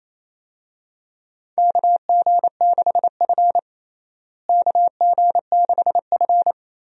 Morse code, heard as a keyed tone, about 27 wpm, 700 Hz. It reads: KG6F KG6F
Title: USA Call Signs - 4 Characters - Repeated 2x - 28wpm